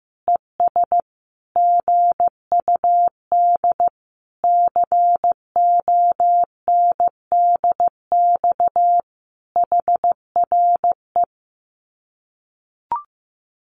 Morse code, heard as a keyed tone, about 15 wpm, 700 Hz. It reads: ES GUD CONDX HRE
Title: Common QSO Phrases 15wpm